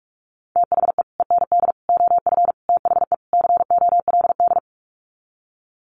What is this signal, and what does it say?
Morse code, heard as a keyed tone, about 44 wpm, 700 Hz. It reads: THE END OF THE COLD
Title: Sets of 5 Words 45wpm